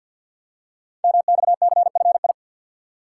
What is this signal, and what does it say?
Morse code, heard as a keyed tone, about 53 wpm, 700 Hz. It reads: MXCFI
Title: Code Groups - 5 Letters - 50wpm